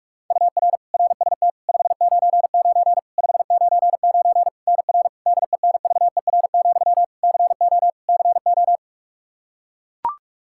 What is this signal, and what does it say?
Morse code, heard as a keyed tone, about 41 wpm, 700 Hz. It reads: UR RST 599 599 NR DENVER, CO CO
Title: Practice QSO - Exchange 1 - Beginning - 45wpm